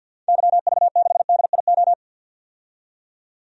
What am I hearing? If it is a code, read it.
Morse code, heard as a keyed tone, about 52 wpm, 700 Hz. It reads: YV6DIO
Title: DXCC-100 Call Signs - 6 Characters - 50wpm